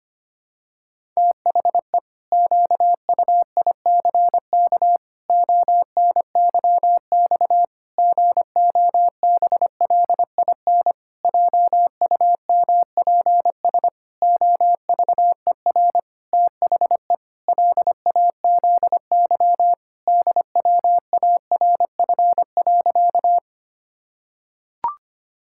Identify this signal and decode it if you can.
Morse code, heard as a keyed tone, about 25 wpm, 700 Hz. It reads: THE QUICK ONYX GOBLIN JUMPS OVER THE LAZY DWARF.
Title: Pangram Challenge v2 - 25wpm